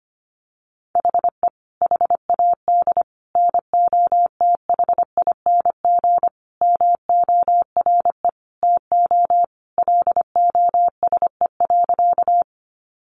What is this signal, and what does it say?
Morse code, heard as a keyed tone, about 25 wpm, 700 Hz. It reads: HE HAD NOTHING MORE TO LOSE.
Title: Sentences from Top 1100 Words 25wpm